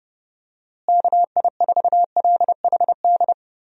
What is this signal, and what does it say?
Morse code, heard as a keyed tone, about 30 wpm, 700 Hz. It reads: KI4LHD
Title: USA Call Signs - 6 Characters - Repeated 2x - 30wpm